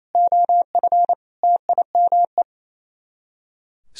Morse code, heard as a keyed tone, about 33 wpm, 700 Hz. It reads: OF TIME
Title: Speed-Racing - Sets of 5 Words 28wpm